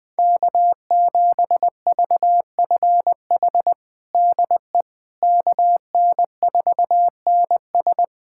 Morse code, heard as a keyed tone, about 20 wpm, 700 Hz. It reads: K7VFH DE KN4NS
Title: USA Callsign DE Callsign - 5 Characters - 20wpm